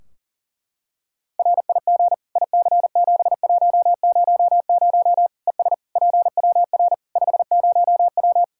Morse code, heard as a keyed tone, about 37 wpm, 700 Hz. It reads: RIG IC7100 ES PWR 50W
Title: Practice QSO - Exchange 2 - Info - 40wpm